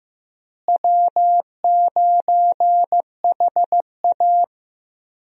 Morse code, heard as a keyed tone, about 16 wpm, 700 Hz. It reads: W9HA
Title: USA Call Signs - 4 Characters - Repeated 3x - 15wpm